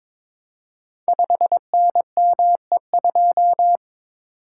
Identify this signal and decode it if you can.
Morse code, heard as a keyed tone, about 22 wpm, 690 Hz. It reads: 5NME2
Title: Code Groups - 5 Letter:Numbers - 22wpm